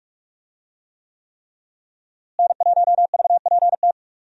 Morse code, heard as a keyed tone, about 46 wpm, 700 Hz. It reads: N1VPT
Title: USA Call Signs - 5 Characters - Repeated 3x - 45wpm